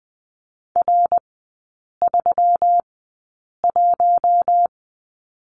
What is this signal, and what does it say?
Morse code, heard as a keyed tone, about 20 wpm, spaced as for 8 wpm, 700 Hz. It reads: R31
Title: 159 Lesson 26e - Letter g - Three Characters 20wpm